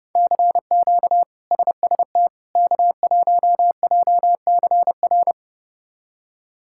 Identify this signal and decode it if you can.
Morse code, heard as a keyed tone, about 32 wpm, 700 Hz. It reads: CQ SST K1JCR
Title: K1USN SST QSO Practice - 30wpm